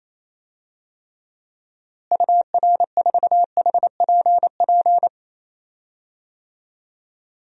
Morse code, heard as a keyed tone, about 28 wpm, 700 Hz. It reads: UR4HPP
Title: DXCC-100 Call Signs - 6 Characters - Repeated 3x - 28wpm